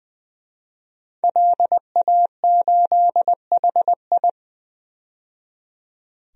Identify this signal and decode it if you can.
Morse code, heard as a keyed tone, about 21 wpm, 700 Hz. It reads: LA8HI
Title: DXCC-100 Call Signs - 5 Characters - 20wpm